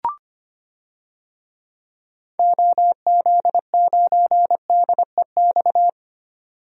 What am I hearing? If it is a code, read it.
Morse code, heard as a keyed tone, about 25 wpm, 700 Hz. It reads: OZ9DEX
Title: DXCC-100 Call Signs - 6 Characters - 25wpm